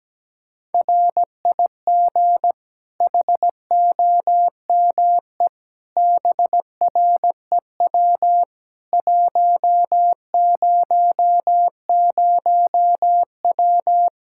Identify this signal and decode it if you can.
Morse code, heard as a keyed tone, about 17 wpm, 700 Hz. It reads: RIG HOME BREW 100W
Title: Practice QSO - Exchange 2 - Info - 17wpm